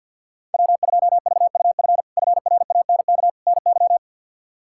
Morse code, heard as a keyed tone, about 51 wpm, 700 Hz. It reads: W2VUF FRANK NY
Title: K1USN SST QSO Practice - 50wpm